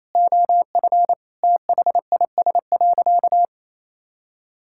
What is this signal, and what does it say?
Morse code, heard as a keyed tone, about 28 wpm, 700 Hz. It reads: OF THIS.
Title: Sentences from Top 1000 Words - Review Part 6 - 28wpm